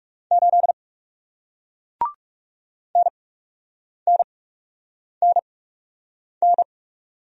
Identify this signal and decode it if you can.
Morse code, heard as a keyed tone, about 41 wpm, spaced as for 8 wpm, 700 Hz. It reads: 8 NNNN